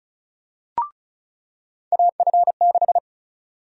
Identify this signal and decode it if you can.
Morse code, heard as a keyed tone, about 36 wpm, 700 Hz. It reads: AF6
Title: USA Call Sign Prefixes - 35wpm